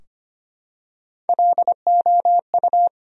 Morse code, heard as a keyed tone, about 25 wpm, 700 Hz. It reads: LOU
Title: USA Common Names - Max 3 Letters - 25wpm